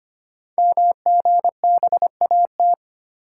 Morse code, heard as a keyed tone, about 25 wpm, 700 Hz. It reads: MGBAT
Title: Code Groups - 5 Letter:Numbers - 25wpm